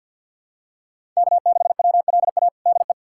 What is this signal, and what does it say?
Morse code, heard as a keyed tone, about 52 wpm, 700 Hz. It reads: K6WLA DE
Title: USA Callsign DE Callsign 50wpm